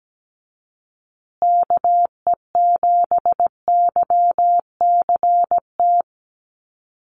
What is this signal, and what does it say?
Morse code, heard as a keyed tone, about 17 wpm, 700 Hz. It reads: KE7YCT
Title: USA Call Signs Repeated 2x 17wpm